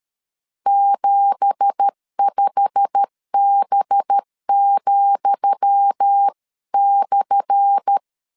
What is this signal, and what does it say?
Morse code, heard as a keyed tone, about 13 wpm, 800 Hz. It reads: 75B,/
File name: LEZIONI / 10 SIGNS / 10D COMA 12wpm 2544